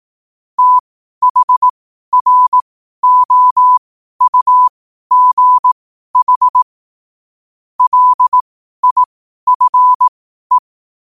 Morse code, heard as a keyed tone, about 18 wpm, spaced as for 12 wpm, 1000 Hz. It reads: THROUGH LIFE
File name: qotdfiles / 12wpm